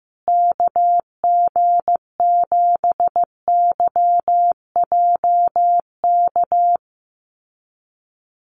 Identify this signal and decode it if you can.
Morse code, heard as a keyed tone, about 15 wpm, 700 Hz. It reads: KG7YJK
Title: USA Call Signs Repeated 3x 15wpm